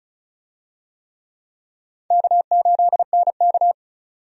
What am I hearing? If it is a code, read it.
Morse code, heard as a keyed tone, about 36 wpm, 700 Hz. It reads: K8NK
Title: USA Call Signs - 4 Characters - 35wpm